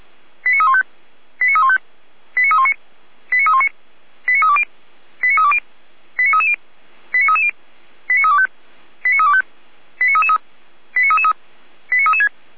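Five-tone selective call (ZVEI), heard as quick runs of five tones, tones of about 70 ms. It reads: 89315 89315 89319 89319 89320 89320 89330 89330 89325 89325 89333 89336
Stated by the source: Received by SDR, ZVEI five-tone call 89337, fire brigade Feuerwehr Orlamünde.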